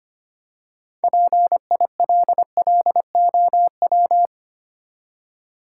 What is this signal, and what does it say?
Morse code, heard as a keyed tone, about 30 wpm, 700 Hz. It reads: PILLOW
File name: Easy Words - Speed-Racing with Hints - 25wpm